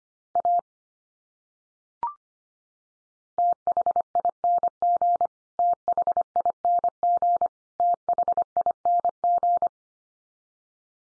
Morse code, heard as a keyed tone, about 25 wpm, 700 Hz. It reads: A THING THING THING
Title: Top 100 Words 25wpm x3